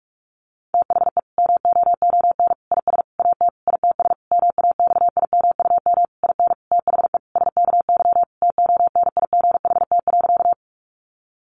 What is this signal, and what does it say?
Morse code, heard as a keyed tone, about 44 wpm, 700 Hz. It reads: THE MOON IS AT ITS MAXIMUM IN THE SKY TONIGHT.
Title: Sentences from Top 1300 Words 45wpm